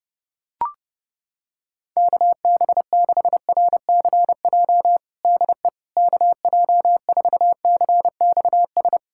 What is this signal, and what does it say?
Morse code, heard as a keyed tone, about 31 wpm, 700 Hz. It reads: KB6RCJ DE KJ4CXS
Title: USA Callsign DE Callsign - 6 Characters - 30wpm